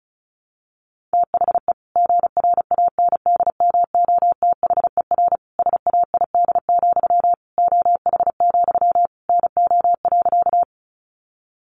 Morse code, heard as a keyed tone, about 34 wpm, 700 Hz. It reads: THE GRANDMOTHER SAID, OH, NO.
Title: Sentences from Top 1200 Words 35wpm